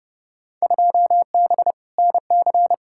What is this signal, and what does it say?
Morse code, heard as a keyed tone, about 30 wpm, 700 Hz. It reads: 2B NC
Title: Field Day Exchanges Encore - 30wpm